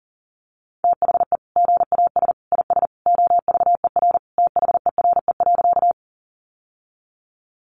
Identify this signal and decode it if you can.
Morse code, heard as a keyed tone, about 40 wpm, 700 Hz. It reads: THE GAS IS OVER THERE.
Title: Sentences from Top 1100 Words 40wpm